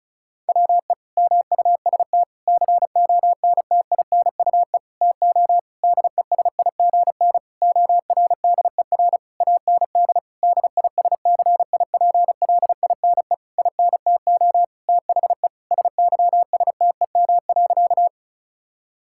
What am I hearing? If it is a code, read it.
Morse code, heard as a keyed tone, about 36 wpm, 690 Hz. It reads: WE MUST CONTINUE TO DESIGN ORDER AND DISCIPLINE INTO THE SYSTEM.
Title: Sentences from Top 1000 Words - Review Part 6 - 35wpm